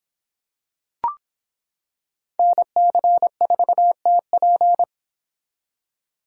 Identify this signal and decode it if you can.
Morse code, heard as a keyed tone, about 26 wpm, 700 Hz. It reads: NC4TP